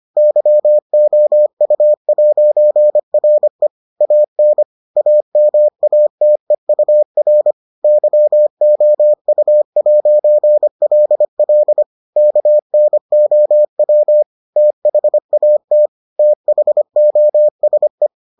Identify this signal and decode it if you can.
Morse code, heard as a keyed tone, about 25 wpm, 600 Hz. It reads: YOU'RE AN AMATEUR YOU'LL KNOW THAT THOSE